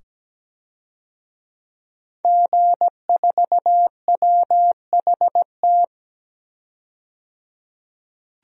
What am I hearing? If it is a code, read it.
Morse code, heard as a keyed tone, about 17 wpm, 700 Hz. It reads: G4WHT
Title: DXCC-100 Call Signs - 5 Characters - 17wpm